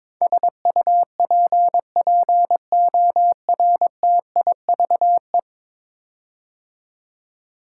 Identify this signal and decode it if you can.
Morse code, heard as a keyed tone, about 22 wpm, 700 Hz. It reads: SUPPORTIVE
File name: practice / 22wpm / 10 Letter Words - First 3 Letters Given - 22wpm